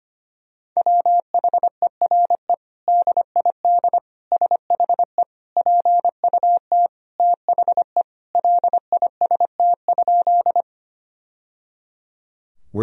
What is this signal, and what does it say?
Morse code, heard as a keyed tone, about 25 wpm, 700 Hz. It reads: WHERE DID SHE PUT THE LIST?
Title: Sentences from Top 1000 Words - Review Part 2 - 25wpm